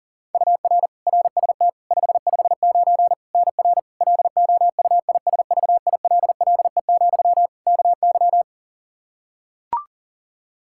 Morse code, heard as a keyed tone, about 37 wpm, 700 Hz. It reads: UR RST 559 NR LOUISVILLE, KY
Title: K1USN SST QSO Practice 40wpm